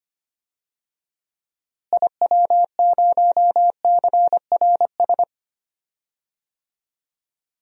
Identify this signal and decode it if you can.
Morse code, heard as a keyed tone, about 25 wpm, 700 Hz. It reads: IW0CRS
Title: DXCC-100 Call Signs - 6 Characters - Repeated 3x - 25wpm